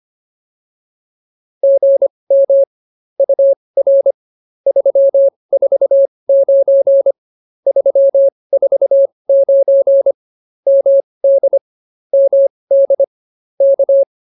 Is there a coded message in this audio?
Yes: Morse code from 1.6 to 14.0 s, heard as a keyed tone, about 24 wpm, spaced as for 20 wpm, 550 Hz. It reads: GM UR 349 349 MD MD K